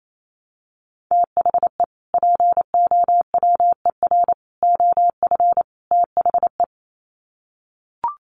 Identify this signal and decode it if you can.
Morse code, heard as a keyed tone, about 28 wpm, 700 Hz. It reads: THE POWER OF THE
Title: Sets of 4 Words 28wpm